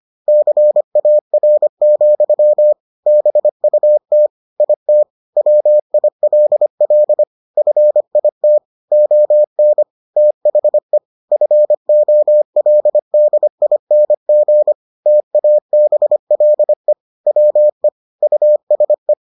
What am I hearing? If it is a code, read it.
Morse code, heard as a keyed tone, about 25 wpm, 600 Hz. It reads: CAR, BUT IT WILL FIT ON THE FOLDING TABLE WE USE